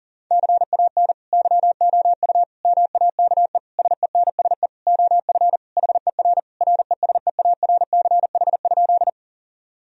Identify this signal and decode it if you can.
Morse code, heard as a keyed tone, about 40 wpm, 700 Hz. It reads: CAN YOU MAKE SENSE OF HER RESEARCH?
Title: Sentences from Top 1000 Words - Review Part 4 - 40wpm